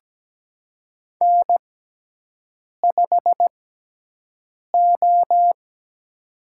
Morse code, heard as a keyed tone, about 17 wpm, spaced as for 6 wpm, 700 Hz. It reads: N5O
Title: USA Call Signs - 4 Characters - 6x Character Spacing - 17wpm